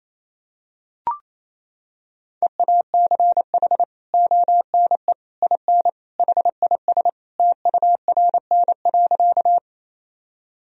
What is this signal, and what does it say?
Morse code, heard as a keyed tone, about 28 wpm, 700 Hz. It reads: EACH ONE IN HIS TURN.
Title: Sentences from Top 1000 Words - Review Part 1 - 28wpm